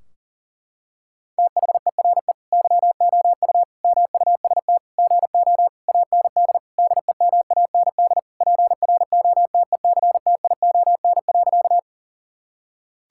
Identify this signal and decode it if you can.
Morse code, heard as a keyed tone, about 40 wpm, 700 Hz. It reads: THERE YOU MUST GO AND DEMAND PROTECTION.
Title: Sentences from Top 1600 Words 40wpm